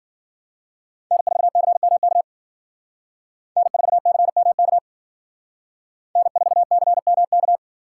Morse code, heard as a keyed tone, about 51 wpm, 700 Hz. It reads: N4CMK N4CMK N4CMK